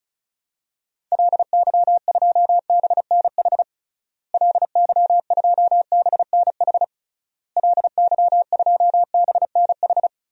Morse code, heard as a keyed tone, about 36 wpm, 700 Hz. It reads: LY2BNH LY2BNH LY2BNH
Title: DXCC-100 Call Signs - 6 Characters - Repeated 3x - 35wpm